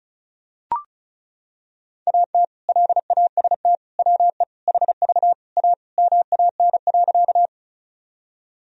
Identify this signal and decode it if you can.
Morse code, heard as a keyed tone, about 34 wpm, 700 Hz. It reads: AT LAST WE HV A MAN.
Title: Sentences from Top 300 Words and Abbreviations 35wpm